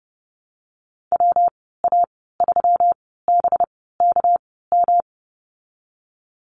Morse code, heard as a keyed tone, about 30 wpm, spaced as for 17 wpm, 700 Hz. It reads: WA3BKM
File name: Lesson 37g - Letter x - Call Signs 30wpm